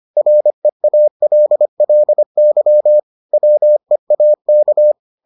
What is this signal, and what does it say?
Morse code, heard as a keyed tone, about 25 wpm, 600 Hz. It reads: REALLY WEAK